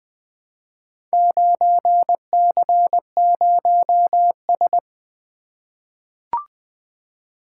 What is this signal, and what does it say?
Morse code, heard as a keyed tone, about 20 wpm, 700 Hz. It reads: 9C0S
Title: Four Letter-Number Combinations - 20wpm